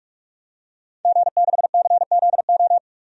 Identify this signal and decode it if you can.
Morse code, heard as a keyed tone, about 43 wpm, 700 Hz. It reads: G6CZO
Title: DXCC-100 Call Signs - 5 Characters - 45wpm